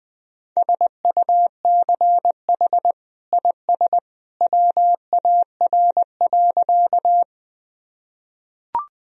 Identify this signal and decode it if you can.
Morse code, heard as a keyed tone, about 21 wpm, 700 Hz. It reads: SUCH IS WAR.
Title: Sentences from Top 1000 Words - Review Part 2 - 20wpm